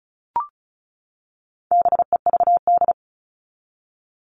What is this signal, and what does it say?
Morse code, heard as a keyed tone, about 34 wpm, 700 Hz. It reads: BEVD
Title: All Cut Numbers - 4-Digit 35wpm